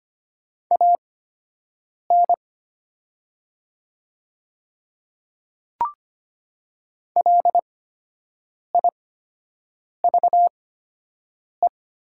Morse code, heard as a keyed tone, about 25 wpm, spaced as for 7 wpm, 700 Hz. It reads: AN LIVE
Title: 8x Character Spacing - Top 300 Words - 25wpm